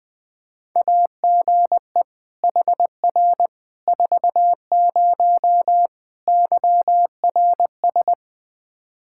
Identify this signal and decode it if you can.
Morse code, heard as a keyed tone, about 21 wpm, 700 Hz. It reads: AGE HR 40 YRS
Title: Practice QSO - Exchange 3 - More Info - 20wpm